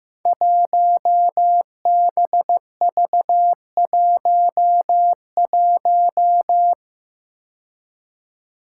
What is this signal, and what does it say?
Morse code, heard as a keyed tone, about 15 wpm, 700 Hz. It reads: 1BV11
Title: Code Groups - 5 Letter:Numbers - 15wpm